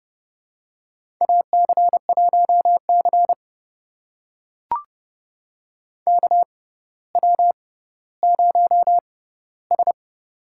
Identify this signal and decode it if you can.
Morse code, heard as a keyed tone, about 30 wpm, spaced as for 15 wpm, 700 Hz. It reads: AC1C KW0S
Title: USA Call Signs - 4 Characters - 6x Character Spacing - 30wpm